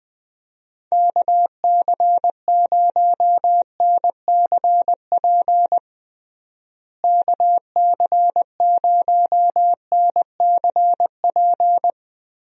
Morse code, heard as a keyed tone, about 20 wpm, 700 Hz. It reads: KC0NCP KC0NCP
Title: USA Call Signs - 6 Characters - Repeated 3x - 20wpm